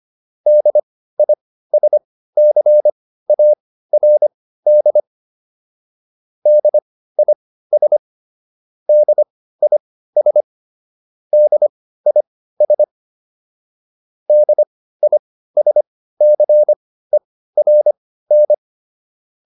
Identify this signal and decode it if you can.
Morse code, heard as a keyed tone, about 25 wpm, spaced as for 15 wpm, 600 Hz. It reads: DISCARD DIS DIS DIS DISCERN